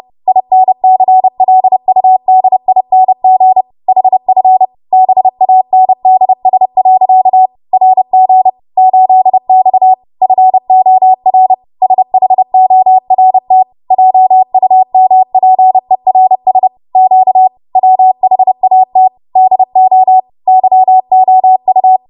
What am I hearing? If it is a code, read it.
Morse code, heard as a keyed tone, about 30 wpm, 750 Hz. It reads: INCLUDING HF BANDS. RG 8X FOR SHORT JUMPERS Q WHAT DO YOU
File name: Morse / 30WPM